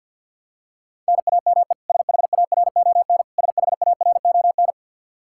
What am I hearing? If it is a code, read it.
Morse code, heard as a keyed tone, about 51 wpm, 700 Hz. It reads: NAME SHARON SHARON
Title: Practice QSO - Exchange 1 - Beginning - 50wpm